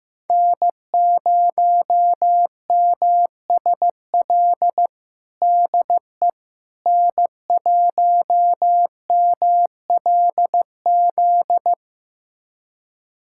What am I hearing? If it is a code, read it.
Morse code, heard as a keyed tone, about 15 wpm, 700 Hz. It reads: N0MSL DE N1MLZ
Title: USA Callsign DE Callsign - 5 Characters - 15wpm